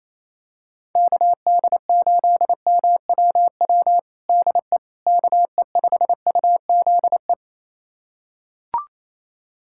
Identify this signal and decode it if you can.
Morse code, heard as a keyed tone, about 28 wpm, 700 Hz. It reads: KD8MWW DE KE5UZE
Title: USA Callsign DE Callsign - 6 Characters - 28wpm